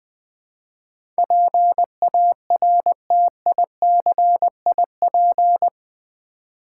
Morse code, heard as a keyed tone, about 20 wpm, 700 Hz. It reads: PARTICIP
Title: Build A Word - 4 Letters At A Time - 11-Letter Words - 20wpm